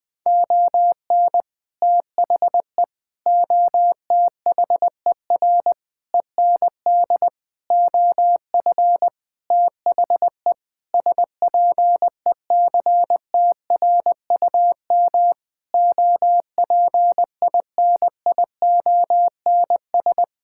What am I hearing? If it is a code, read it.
Morse code, heard as a keyed tone, about 20 wpm, 700 Hz. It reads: ON THE OTHER END OF THE SPECTRUM OPINIONS
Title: Sentences with 7-Word Phrases - 20wpm